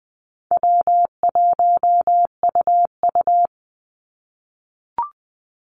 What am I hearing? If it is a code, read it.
Morse code, heard as a keyed tone, about 20 wpm, 700 Hz. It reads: W1UU
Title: Easy Call Signs - 20wpm